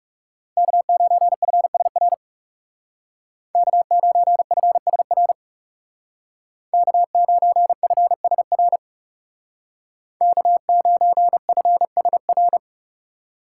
Morse code, heard as a keyed tone, about 35 wpm, 700 Hz. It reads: K9FSR K9FSR K9FSR K9FSR
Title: Speed Racing - USA 1x3 Call Signs 30wpm